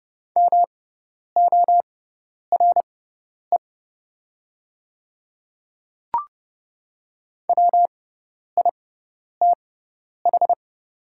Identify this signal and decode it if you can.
Morse code, heard as a keyed tone, about 30 wpm, spaced as for 10 wpm, 700 Hz. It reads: MORE WITH
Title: 6x Character Spacing - Top 200 Words - 30wpm